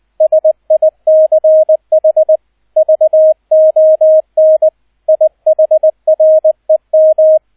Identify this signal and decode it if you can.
Morse code, heard as a keyed tone, about 20 wpm, 630 Hz, with radio noise. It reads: SICH VON IHREM
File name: 002 Zweiter Streich 20wpm